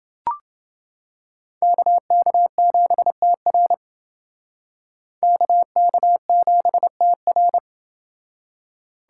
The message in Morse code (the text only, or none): KK7TR KK7TR